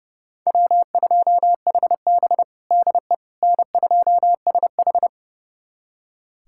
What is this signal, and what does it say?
Morse code, heard as a keyed tone, about 30 wpm, 700 Hz. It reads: W2HB DE N2SH
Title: USA Callsign DE Callsign - 4 Characters - 30wpm